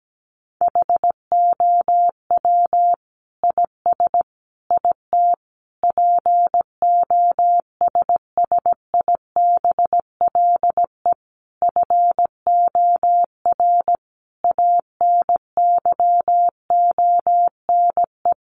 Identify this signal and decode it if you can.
Morse code, heard as a keyed tone, about 17 wpm, 700 Hz. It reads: HOW IS IT POSSIBLE FOR ANYONE
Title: Sentences from Top 900 Words 17wpm